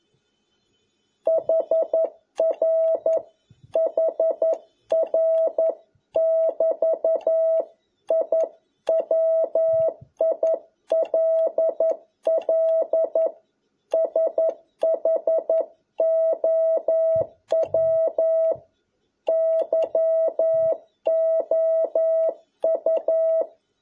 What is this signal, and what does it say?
Morse code, heard as a keyed tone, about 11 wpm, 650 Hz. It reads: HR HR <BT> I WILL SHOW YOU
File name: jh8cbh / 2021cw